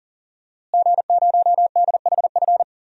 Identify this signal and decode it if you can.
Morse code, heard as a keyed tone, about 39 wpm, 700 Hz. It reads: G0DHF
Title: DXCC-100 Call Signs - 5 Characters - Repeated 2x - 40wpm